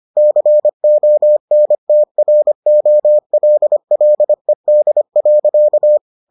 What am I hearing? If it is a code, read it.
Morse code, heard as a keyed tone, about 25 wpm, 600 Hz. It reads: CONTROLLED.